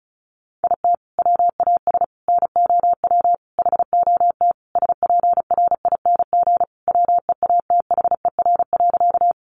Morse code, heard as a keyed tone, about 34 wpm, 700 Hz. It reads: IT WAS NOW HOT SPRING WEATHER.
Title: Sentences from Top 1200 Words 35wpm